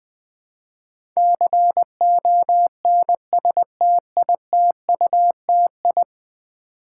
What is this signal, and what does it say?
Morse code, heard as a keyed tone, about 20 wpm, 700 Hz. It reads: CONSTITUTI
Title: Build A Word - 5 Letters At A Time - 12-Letter Words - 20wpm